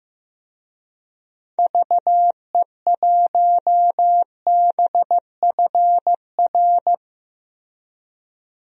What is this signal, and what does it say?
Morse code, heard as a keyed tone, about 15 wpm, 700 Hz. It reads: VE1BFR
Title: DXCC-100 Call Signs - 6 Characters - Repeated 2x - 15wpm